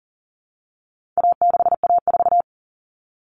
Morse code, heard as a keyed tone, about 38 wpm, 700 Hz. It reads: A6A4
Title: All Cut Numbers - 4-Digit 40wpm